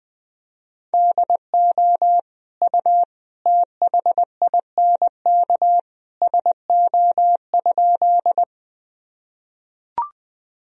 Morse code, heard as a keyed tone, about 20 wpm, 700 Hz. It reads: DO U THINK SO?
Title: Sentences from Top 200 Words and Abbreviations 20wpm